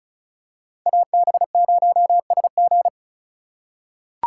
Morse code, heard as a keyed tone, about 36 wpm, 700 Hz. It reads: AB0SG
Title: USA Call Signs - 5 Characters - 4x Character Spacing - 35wpm